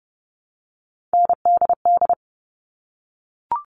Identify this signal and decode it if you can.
Morse code, heard as a keyed tone, about 30 wpm, 700 Hz. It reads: NDD
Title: All Cut Numbers - 3-Digit 30wpm